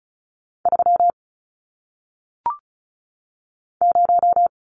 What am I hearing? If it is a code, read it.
Morse code, heard as a keyed tone, about 36 wpm, 700 Hz. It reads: 3 0
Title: Numbers - 1-Digit 35wpm